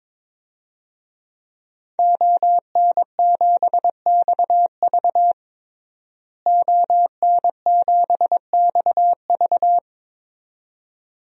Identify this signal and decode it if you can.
Morse code, heard as a keyed tone, about 22 wpm, 700 Hz. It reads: ON7XV ON7XV
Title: DXCC-100 Call Signs - 5 Characters - Repeated 2x - 22wpm